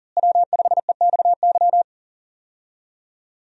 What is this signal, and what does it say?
Morse code, heard as a keyed tone, about 37 wpm, 700 Hz. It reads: W5EXY
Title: USA Call Signs - 5 Characters - Repeated 2x - 40wpm